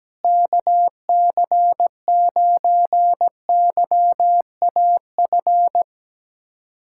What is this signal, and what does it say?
Morse code, heard as a keyed tone, about 17 wpm, 700 Hz. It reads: KC9YAF
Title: USA Callsign DE Callsign 17wpm